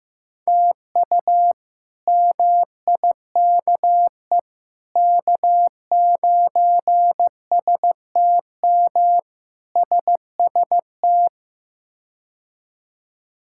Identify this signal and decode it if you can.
Morse code, heard as a keyed tone, about 15 wpm, 700 Hz. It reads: TU MIKE K9STM SST
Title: K1USN SST QSO Practice - 15wpm